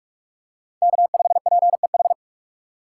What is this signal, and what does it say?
Morse code, heard as a keyed tone, about 46 wpm, 700 Hz. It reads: K5PEH
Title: USA Call Signs - 5 Characters - 2x Character Spacing - 45wpm